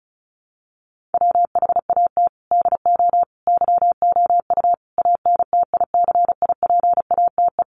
Morse code, heard as a keyed tone, about 34 wpm, 700 Hz. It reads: WHAT DO YOU ANTICIPATE
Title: Sentences from Top 1500 Words 35wpm